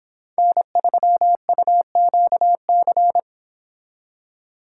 Morse code, heard as a keyed tone, about 25 wpm, 700 Hz. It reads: N3UQC